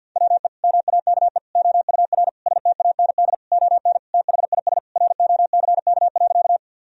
Morse code, heard as a keyed tone, about 50 wpm, 700 Hz. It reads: WE MAKE OUR STAND ON THIS ROCK.
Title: Sentences from Top 1000 Words - Review Part 2 - 50wpm